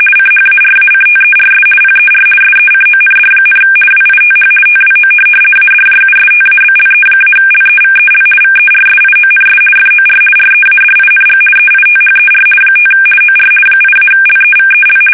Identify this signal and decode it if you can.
RTTY, about 50 baud. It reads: CHARACTERS PER SECOND, WITH TWO ADJUSTABLE-LEVEL VERSIONS FOR READING EITHER 5 OR 8-LEVEL
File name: fsk850-50B / teletype-corp